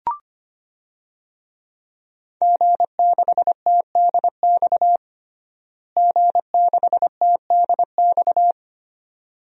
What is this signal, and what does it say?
Morse code, heard as a keyed tone, about 25 wpm, 700 Hz. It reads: G6TDX G6TDX
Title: DXCC-100 Call Signs - 5 Characters - Repeated 2x - 25wpm